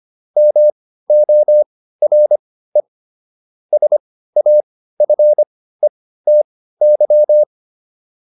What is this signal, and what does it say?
Morse code, heard as a keyed tone, about 25 wpm, spaced as for 15 wpm, 600 Hz. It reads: MORE SAFETY